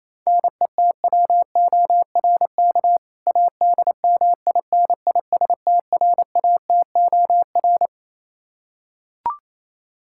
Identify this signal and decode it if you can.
Morse code, heard as a keyed tone, about 27 wpm, 700 Hz. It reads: NETWORK ADMINISTRATOR
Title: K1USN SST QSO Practice 28wpm